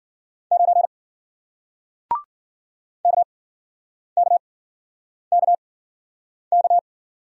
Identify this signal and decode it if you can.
Morse code, heard as a keyed tone, about 51 wpm, spaced as for 9 wpm, 700 Hz. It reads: 9 KKKK